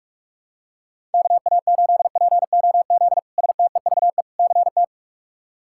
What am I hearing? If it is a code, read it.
Morse code, heard as a keyed tone, about 43 wpm, 700 Hz. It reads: KA8POZ STEVE CT
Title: K1USN SST QSO Practice - 45wpm